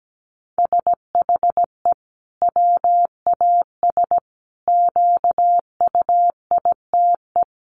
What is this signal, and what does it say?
Morse code, heard as a keyed tone, about 17 wpm, 700 Hz. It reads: SHE WAS QUITE
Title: Sentences from Top 1500 Words 17wpm